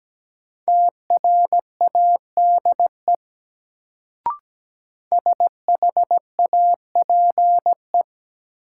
Morse code, heard as a keyed tone, about 17 wpm, 700 Hz. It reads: TRADE SHAPE
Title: Sets of 5 Words Encore - New Word Drill - 17wpm